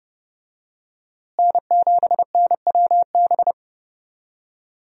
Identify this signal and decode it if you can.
Morse code, heard as a keyed tone, about 30 wpm, 700 Hz. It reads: N7NWB
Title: USA Call Signs - 5 Characters - Repeated 2x - 30wpm